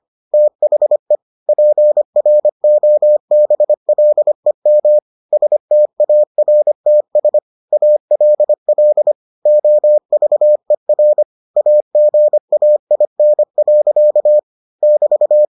Morse code, heard as a keyed tone, about 25 wpm, 600 Hz. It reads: THE PROBLEM STARTS ALL OVER AGAIN. <BT>